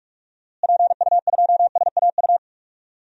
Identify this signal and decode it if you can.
Morse code, heard as a keyed tone, about 46 wpm, 700 Hz. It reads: PU2SAU
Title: DXCC-100 Call Signs - 6 Characters - Repeated 2x - 45wpm